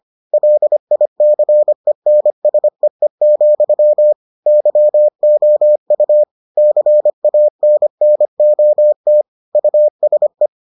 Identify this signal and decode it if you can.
Morse code, heard as a keyed tone, about 25 wpm, 600 Hz. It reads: LICENSEE, YOU CANNOT USE